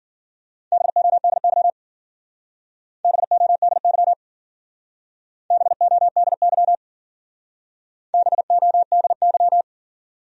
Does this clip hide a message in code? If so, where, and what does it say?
Yes: Morse code from 0.7 to 9.6 s, heard as a keyed tone, about 53 wpm, 700 Hz. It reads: BODY BODY BODY BODY